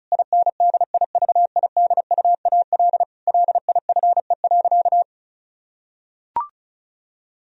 Morse code, heard as a keyed tone, about 34 wpm, 700 Hz. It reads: INDIVIDUAL LIFE.
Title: Sentences from Top 1000 Words - Review Part 4 - 35wpm